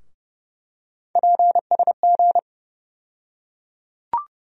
Morse code, heard as a keyed tone, about 32 wpm, 700 Hz. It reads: PSG
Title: USA Call Sign Postfixes - 3 Letters 30wpm